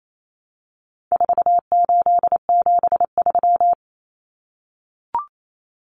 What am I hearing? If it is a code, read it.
Morse code, heard as a keyed tone, about 28 wpm, 700 Hz. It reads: 4873
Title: Common Cut Numbers - 4-Digit 28wpm